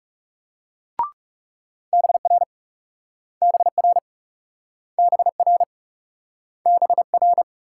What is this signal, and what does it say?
Morse code, heard as a keyed tone, about 38 wpm, 700 Hz. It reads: BR BR BR BR